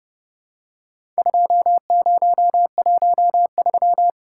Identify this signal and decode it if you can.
Morse code, heard as a keyed tone, about 30 wpm, 700 Hz. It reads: 2013
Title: Worked All Europe DX Contest - QTC Exchange - 30wpm